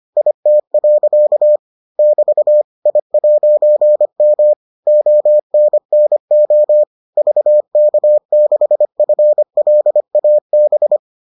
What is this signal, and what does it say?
Morse code, heard as a keyed tone, about 25 wpm, 600 Hz. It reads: IT. <BT> I'M ONNO VK6FLAB